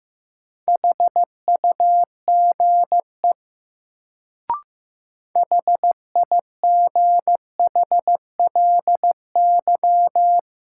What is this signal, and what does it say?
Morse code, heard as a keyed tone, about 15 wpm, 700 Hz. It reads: HUGE HIGHLY
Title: Sentences from Top 1100 Words - New Word Drill Alt - 15wpm